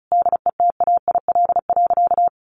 Morse code, heard as a keyed tone, about 34 wpm, 700 Hz. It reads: DETAIL.
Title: Sentences from Top 1500 Words 35wpm